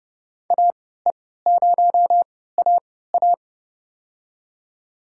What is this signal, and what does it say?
Morse code, heard as a keyed tone, about 31 wpm, spaced as for 17 wpm, 700 Hz. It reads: AE0AA
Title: USA Call Signs - 5 Characters - 3x Character Spacing - 30wpm